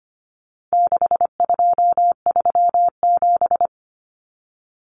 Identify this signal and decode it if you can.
Morse code, heard as a keyed tone, about 25 wpm, 700 Hz. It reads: 6237
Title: Numbers - 4-Digit 25wpm